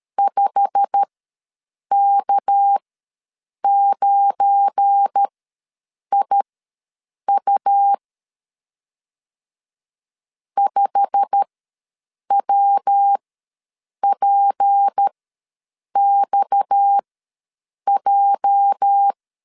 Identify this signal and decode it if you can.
Morse code, heard as a keyed tone, about 13 wpm, spaced as for 7 wpm, 800 Hz. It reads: 5K9IU 5WPXJ
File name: LEZIONI / 8 YJ45 / 8 YJ45 ws 12wpm 3146